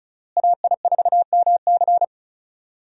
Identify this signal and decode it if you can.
Morse code, heard as a keyed tone, about 36 wpm, 700 Hz. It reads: AI4MC
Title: USA Call Signs - 5 Characters - Repeated 3x - 35wpm